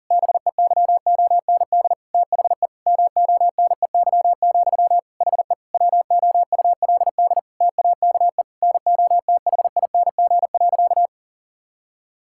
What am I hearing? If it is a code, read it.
Morse code, heard as a keyed tone, about 40 wpm, 700 Hz. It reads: BEYOND THE MONEY, HE WOULD TAKE NOTHING.
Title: Sentences from Top 1000 Words - Review Part 6 - 40wpm